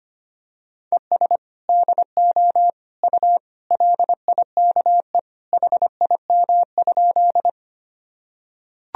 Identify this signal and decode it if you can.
Morse code, heard as a keyed tone, about 25 wpm, 700 Hz. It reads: ES DO U LIKE HIM?
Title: Sentences from Top 100 Words and Abbreviations 25wpm